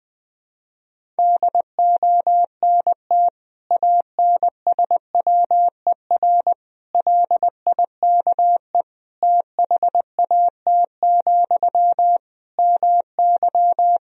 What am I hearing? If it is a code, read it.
Morse code, heard as a keyed tone, about 20 wpm, 700 Hz. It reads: DONT ANSWER LIKE THAT, MY
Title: Sentences from Top 400 Words and Abbreviations 20wpm